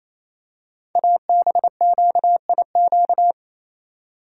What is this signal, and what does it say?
Morse code, heard as a keyed tone, about 29 wpm, 700 Hz. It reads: ABQIQ
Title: Code Groups - 5 Letters - 28wpm